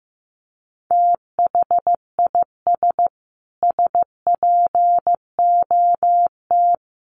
Morse code, heard as a keyed tone, about 15 wpm, 700 Hz. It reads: THIS SPOT
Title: Sentences from Top 1100 Words 15wpm